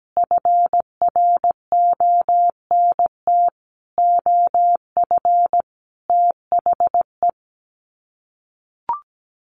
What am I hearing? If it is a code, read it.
Morse code, heard as a keyed tone, about 17 wpm, 700 Hz. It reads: FRONT OF THE
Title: Sets of 5 Words Encore2 17wpm